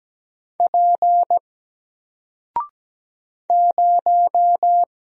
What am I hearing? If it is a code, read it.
Morse code, heard as a keyed tone, about 17 wpm, 700 Hz. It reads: P 0
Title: Single Character (Letter:Number:Punctuation:Prosign) 17wpm